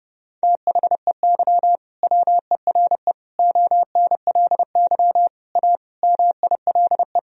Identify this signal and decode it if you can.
Morse code, heard as a keyed tone, about 30 wpm, 700 Hz. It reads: THEY WERE ONLY A MILE
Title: Sentences from Top 1000 Words - Review Part 1 - 30wpm